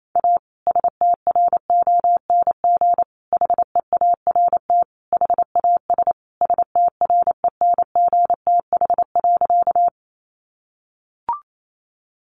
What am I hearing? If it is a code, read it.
Morse code, heard as a keyed tone, about 28 wpm, 700 Hz. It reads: A STRONG HEART HAS STRENGTH.
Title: Sentences from Top 1300 Words 28wpm